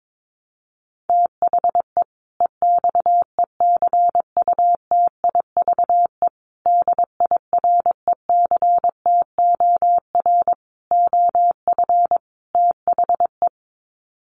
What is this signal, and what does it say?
Morse code, heard as a keyed tone, about 22 wpm, 700 Hz. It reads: THE EXECUTIVE DIRECTOR OF THE
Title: Sets of 5 Words Encore 22wpm